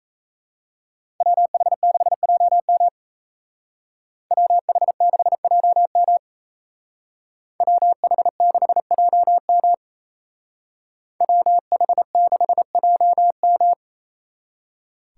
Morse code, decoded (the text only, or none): WH6JM WH6JM WH6JM WH6JM